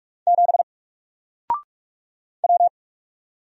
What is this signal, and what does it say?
Morse code, heard as a keyed tone, about 44 wpm, 700 Hz. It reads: 7 W
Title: Single Letter-Number 45wpm